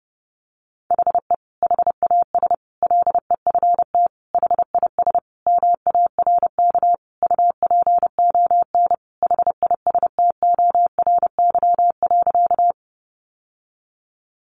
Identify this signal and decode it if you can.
Morse code, heard as a keyed tone, about 30 wpm, 700 Hz. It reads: HE HAS LEFT HIS MARK UPON HISTORY.
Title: Sentences from Top 1100 Words 30wpm